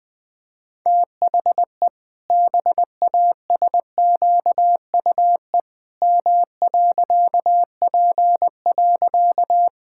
Morse code, heard as a keyed tone, about 20 wpm, 700 Hz. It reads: THE BASQUE M.P.
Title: Pangram Challenge v2 - 20wpm